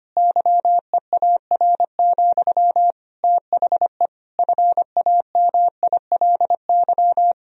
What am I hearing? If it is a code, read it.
Morse code, heard as a keyed tone, about 25 wpm, 700 Hz. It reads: YEAR, THE FAMILY
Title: Sentences from Top 1000 Words - Review Part 2 - 25wpm